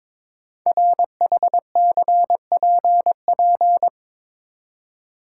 Morse code, heard as a keyed tone, about 22 wpm, 700 Hz. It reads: RHCPP
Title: Code Groups - 5 Letters - 22wpm